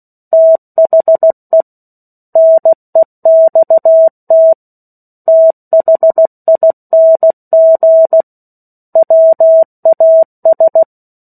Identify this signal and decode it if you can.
Morse code, heard as a keyed tone, about 16 wpm, 650 Hz. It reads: THE NEXT THING WAS
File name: 03-Chapter-03 16 WPM